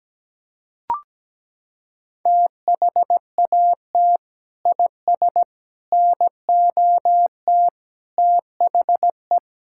Morse code, heard as a keyed tone, about 17 wpm, 700 Hz. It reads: THAT IS NOT THE
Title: Sentences from Top 1000 Words - Review Part 5 - 17wpm